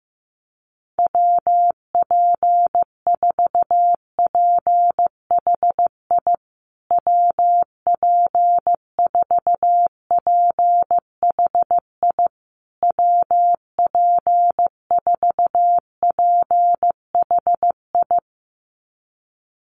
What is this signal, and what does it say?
Morse code, heard as a keyed tone, about 15 wpm, 700 Hz. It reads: WP4PHI WP4PHI WP4PHI
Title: USA Call Signs Repeated 3x 15wpm